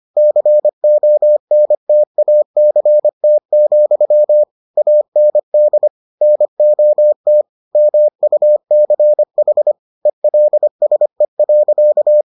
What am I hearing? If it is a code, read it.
Morse code, heard as a keyed tone, about 25 wpm, 600 Hz. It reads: CONTACT, AND NOT MUCH ELSE.